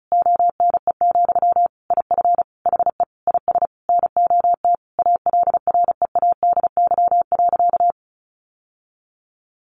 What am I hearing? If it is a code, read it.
Morse code, heard as a keyed tone, about 34 wpm, 700 Hz. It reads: ONE, IF HE IS NOT ALREADY.
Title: Sentences from Top 900 Words 35wpm